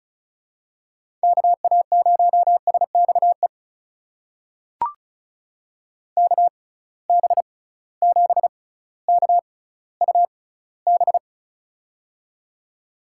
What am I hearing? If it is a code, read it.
Morse code, heard as a keyed tone, about 36 wpm, spaced as for 18 wpm, 700 Hz. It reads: KA0SXE KB7KUB
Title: USA Call Signs - 6x Character Spacing - 35wpm